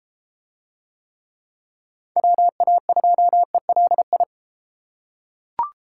Morse code, heard as a keyed tone, about 30 wpm, 700 Hz. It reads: WA2ELI